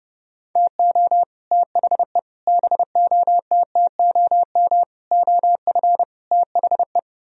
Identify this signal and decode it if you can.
Morse code, heard as a keyed tone, about 30 wpm, 700 Hz. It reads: TO THE BOTTOM OF THE